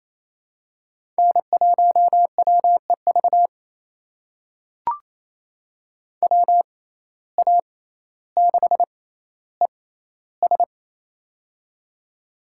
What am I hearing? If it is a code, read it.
Morse code, heard as a keyed tone, about 28 wpm, spaced as for 14 wpm, 700 Hz. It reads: N1WEV WA6ES
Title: USA Call Signs - 5 Characters - 6x Character Spacing - 28wpm